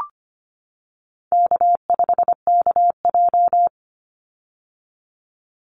Morse code, heard as a keyed tone, about 25 wpm, 700 Hz. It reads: K5KJ
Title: Call Signs - 25wpm